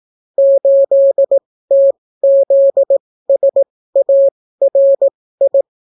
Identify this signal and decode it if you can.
Morse code, heard as a keyed tone, about 18 wpm, spaced as for 14 wpm, 550 Hz. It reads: 8TZSARI